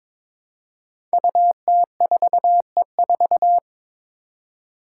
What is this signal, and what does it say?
Morse code, heard as a keyed tone, about 22 wpm, 700 Hz. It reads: UT4E4
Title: Five Challenging Letter-Number Combinations 22wpm